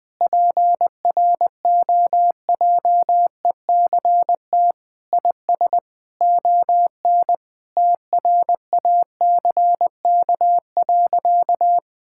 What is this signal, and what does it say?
Morse code, heard as a keyed tone, about 21 wpm, 700 Hz. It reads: PROJECT IS ON TRACK.
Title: Sentences with 6-Word Phrases - Spoken Beginning - 20wpm